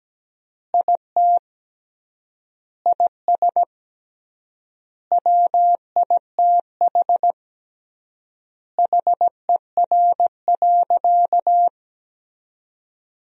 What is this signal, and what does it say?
Morse code, heard as a keyed tone, about 17 wpm, 700 Hz. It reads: IT IS WITH HER.
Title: 3x Word-Spacing - Sentences from Top 100 Words - 17wpm